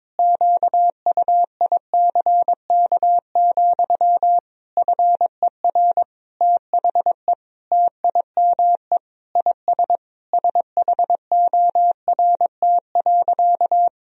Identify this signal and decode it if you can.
Morse code, heard as a keyed tone, about 22 wpm, 700 Hz. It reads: QUICK, FER THE TIME IS SHORT.
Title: Sentences from Top 500 Words and Abbreviations 22wpm